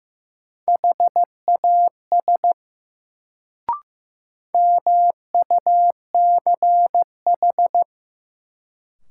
Morse code, heard as a keyed tone, about 15 wpm, 700 Hz. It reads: HAS MUCH
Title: Sets of 2 Words - New Word Drill - 15wpm